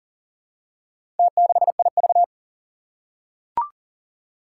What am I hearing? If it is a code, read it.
Morse code, heard as a keyed tone, about 37 wpm, 700 Hz. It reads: T6IV
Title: Four Challenging Letter-Number Combinations 40wpm